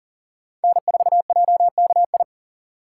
Morse code, heard as a keyed tone, about 40 wpm, 700 Hz. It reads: N4JKI
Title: USA Call Signs - 5 Characters - 2x Character Spacing - 40wpm